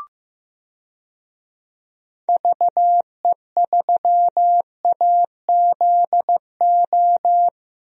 Morse code, heard as a keyed tone, about 15 wpm, 700 Hz. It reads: VE3AZO
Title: DXCC-100 Call Signs - 6 Characters - Repeated 2x - 15wpm